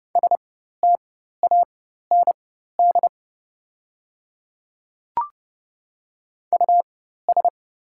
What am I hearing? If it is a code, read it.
Morse code, heard as a keyed tone, about 30 wpm, spaced as for 14 wpm, 690 Hz. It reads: STAND US
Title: 4x Character Spacing - Top 500 Words - 30wpm